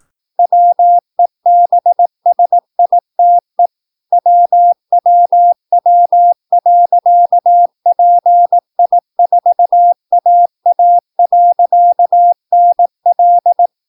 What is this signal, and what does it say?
Morse code, heard as a keyed tone, about 18 wpm, 700 Hz. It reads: WEBSITE WWW.PI4AA.NL
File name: PI4AA Uitzending